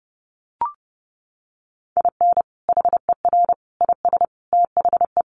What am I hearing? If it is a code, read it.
Morse code, heard as a keyed tone, about 30 wpm, 700 Hz. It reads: IN HER IS THE
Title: Sentences from Top 1400 Words 30wpm